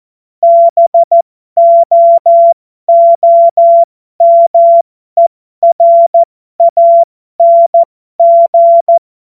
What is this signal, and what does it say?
Morse code, heard as a keyed tone, about 14 wpm, spaced as for 12 wpm, 680 Hz. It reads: BOOMERANG
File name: Morsekurs / Witze